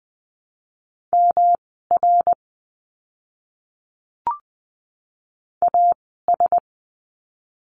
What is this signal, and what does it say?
Morse code, heard as a keyed tone, about 20 wpm, spaced as for 14 wpm, 700 Hz. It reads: MR AS
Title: 244 Lesson 39f - Letter z - Words 20wpm